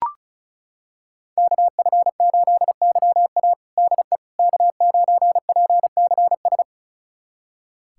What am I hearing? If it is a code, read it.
Morse code, heard as a keyed tone, about 36 wpm, 700 Hz. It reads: KF8YA DE K9PCS
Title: USA Callsign DE Callsign - 5 Characters - 35wpm